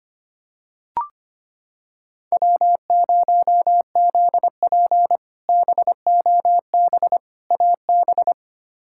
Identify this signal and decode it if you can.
Morse code, heard as a keyed tone, about 25 wpm, 700 Hz. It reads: W0ZP BOB AB
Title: K1USN SST QSO Practice - 25wpm